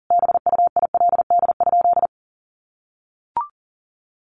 Morse code, heard as a keyed tone, about 40 wpm, 700 Hz. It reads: BUILD?
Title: Sentences from Top 1300 Words 40wpm